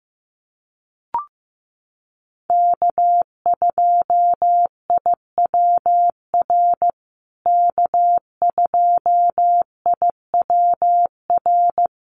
Morse code, heard as a keyed tone, about 14 wpm, 700 Hz. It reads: K2IWR K2IWR
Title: USA Call Signs Repeated 3x 15wpm